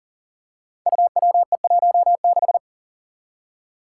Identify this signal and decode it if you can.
Morse code, heard as a keyed tone, about 40 wpm, 700 Hz. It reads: UWE16
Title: Five Challenging Letter-Number Combinations 40wpm